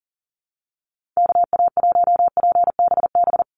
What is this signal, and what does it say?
Morse code, heard as a keyed tone, about 40 wpm, 700 Hz. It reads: KA1PBB
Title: USA Call Signs 40wpm